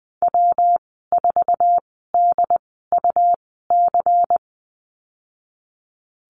Morse code, heard as a keyed tone, about 19 wpm, spaced as for 14 wpm, 700 Hz. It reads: W4DUC
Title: 186 Lesson 30g - Letter b - Call Signs 20wpm